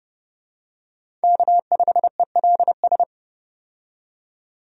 Morse code, heard as a keyed tone, about 30 wpm, 700 Hz. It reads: K5ELS
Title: USA Call Signs - 5 Characters - 30wpm